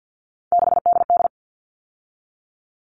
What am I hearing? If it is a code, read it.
Morse code, heard as a keyed tone, about 50 wpm, 700 Hz. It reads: 6DD
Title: All Cut Numbers - 3-Digit 50wpm